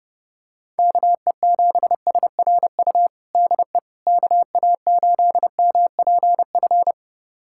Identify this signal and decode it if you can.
Morse code, heard as a keyed tone, about 30 wpm, 700 Hz. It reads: KE7SRU DE KA8MPF
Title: USA Callsign DE Callsign 30wpm